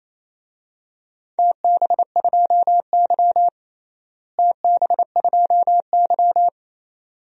Morse code, heard as a keyed tone, about 28 wpm, 700 Hz. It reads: TB2Y TB2Y
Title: DXCC-100 Call Signs - 4 Characters - Repeated 2x - 28wpm